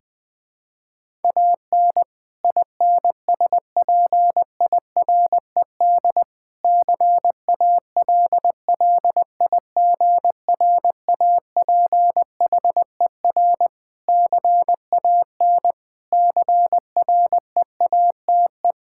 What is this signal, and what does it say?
Morse code, heard as a keyed tone, about 21 wpm, 700 Hz. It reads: AN INSPIRED CALLIGRAPHER CAN CREATE
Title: Pangram Challenge v2 - 20wpm